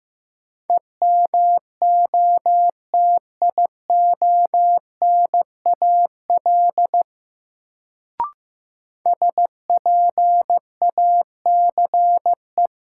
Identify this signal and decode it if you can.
Morse code, heard as a keyed tone, about 15 wpm, 700 Hz. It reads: EMOTIONAL SPACE
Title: Sentences from Top 1000 Words - Review Part 5 - New Word Drill - 15wpm